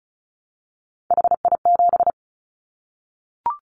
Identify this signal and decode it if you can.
Morse code, heard as a keyed tone, about 36 wpm, 700 Hz. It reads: HI7
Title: Callsign Prefixes for Top 100 DXCC - 35wpm